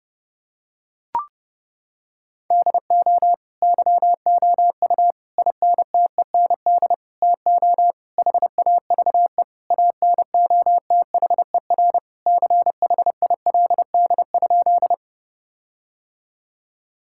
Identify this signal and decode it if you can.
Morse code, heard as a keyed tone, about 30 wpm, 700 Hz. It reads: DO YOU INTEND TO HAVE ANOTHER CHILD?
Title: Sentences from Top 1600 Words 30wpm